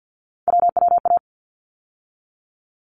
Morse code, heard as a keyed tone, about 48 wpm, 700 Hz. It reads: WWA
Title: ARRL Sections 50wpm